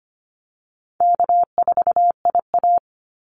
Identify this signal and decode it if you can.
Morse code, heard as a keyed tone, about 25 wpm, 700 Hz. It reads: K4IA
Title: Easy Call Signs - 25wpm